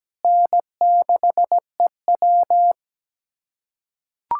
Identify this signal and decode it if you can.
Morse code, heard as a keyed tone, about 17 wpm, 700 Hz. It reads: N6EW
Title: USA Callsign DE Callsign - 4 Characters - 17wpm